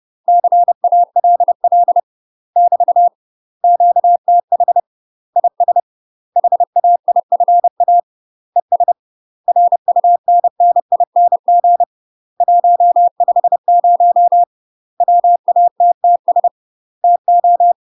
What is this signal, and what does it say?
Morse code, heard as a keyed tone, about 30 wpm, 700 Hz. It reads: CALL <BT> QTH IS HAIFA ES RUNNING 150 WATTS TO
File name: UK MASTER QSOs TRAINING / 30WPM / UK MASTER QSOs 0024